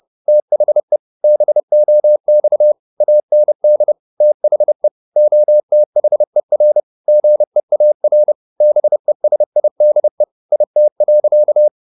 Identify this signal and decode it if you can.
Morse code, heard as a keyed tone, about 30 wpm, 600 Hz. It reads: THE BOX AND THE OTHER GEAR BESIDE IT.
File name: the old man and the sea / the old man and the sea 1 30wpm